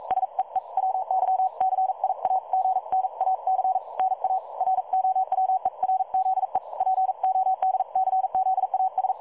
Morse code, heard as a keyed tone, about 42 wpm, 760 Hz. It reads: SEE WYCIAGNAC NA ROWERZE PODCZAS
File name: dlazaawansowanych / nagraniazpasma / websdr recording 3565.0kHz